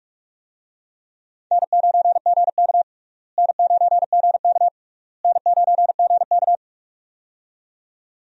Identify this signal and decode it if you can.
Morse code, heard as a keyed tone, about 46 wpm, 700 Hz. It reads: N9GK N9GK N9GK
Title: USA Call Signs - 4 Characters - Repeated 3x - 45wpm